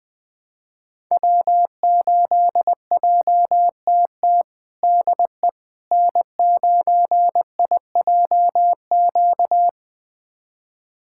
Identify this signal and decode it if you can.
Morse code, heard as a keyed tone, about 20 wpm, 700 Hz. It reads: W8JTT DE N9IJQ
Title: USA Callsign DE Callsign - 5 Characters - 20wpm